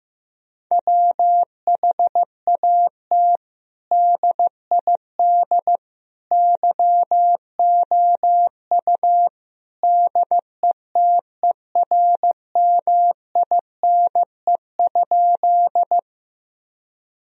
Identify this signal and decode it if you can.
Morse code, heard as a keyed tone, about 15 wpm, 700 Hz. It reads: WHAT DID YOU DETERMINE?
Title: Sentences from Top 1000 Words - Review Part 4 - 15wpm